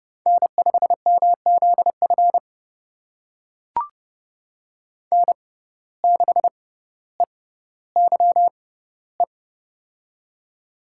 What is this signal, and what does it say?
Morse code, heard as a keyed tone, about 30 wpm, spaced as for 15 wpm, 700 Hz. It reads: N5MZF N6EYE
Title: USA Call Signs - 5 Characters - 6x Character Spacing - 30wpm